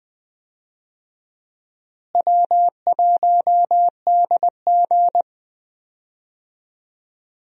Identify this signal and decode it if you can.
Morse code, heard as a keyed tone, about 20 wpm, 700 Hz. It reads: W1DG
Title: USA Call Signs - 4 Characters - Repeated 3x - 20wpm